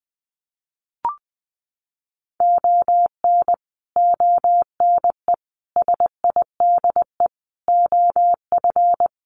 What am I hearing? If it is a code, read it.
Morse code, heard as a keyed tone, about 20 wpm, 700 Hz. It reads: ON ONE SIDE OF
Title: Sets of 5 Words 20wpm